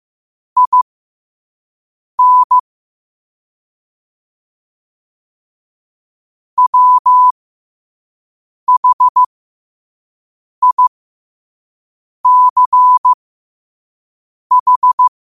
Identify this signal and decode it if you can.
Morse code, heard as a keyed tone, about 15 wpm, spaced as for 5 wpm, 1000 Hz. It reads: IN WHICH
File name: qotdfiles / 5wpm